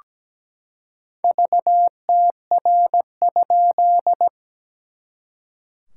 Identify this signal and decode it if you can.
Morse code, heard as a keyed tone, about 17 wpm, 700 Hz. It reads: VTR?
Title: SOTA QSO Practice 17wpm